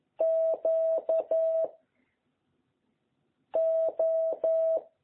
Morse code, heard as a keyed tone, about 11 wpm, 660 Hz. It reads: Q O
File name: jh8cbh / 2021cw